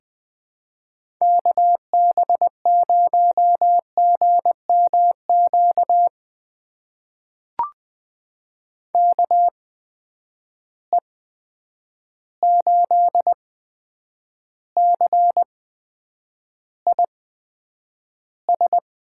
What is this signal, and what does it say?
Morse code, heard as a keyed tone, about 20 wpm, spaced as for 9 wpm, 700 Hz. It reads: KB0GMQ KE8CIS
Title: USA Call Signs - 6 Characters - 8x Character Spacing - 20wpm